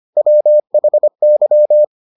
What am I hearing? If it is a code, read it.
Morse code, heard as a keyed tone, about 25 wpm, 600 Hz. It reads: WHY